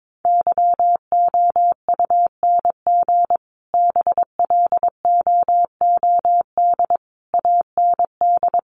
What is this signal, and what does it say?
Morse code, heard as a keyed tone, about 22 wpm, 700 Hz. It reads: YOUNG BLOOD AND
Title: Sentences from Top 1100 Words 22wpm